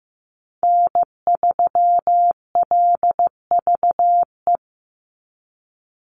Morse code, heard as a keyed tone, about 15 wpm, 700 Hz. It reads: N3LVE
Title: USA Call Signs 15wpm